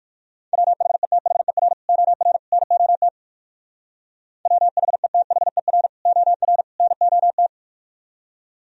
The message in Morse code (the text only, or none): WHETHER OR NOT WHETHER OR NOT